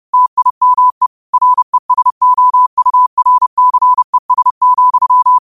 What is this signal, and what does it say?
Morse code, heard as a keyed tone, about 30 wpm, 1000 Hz. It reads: TIME RESOURCES,
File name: qotdfiles / 30wpm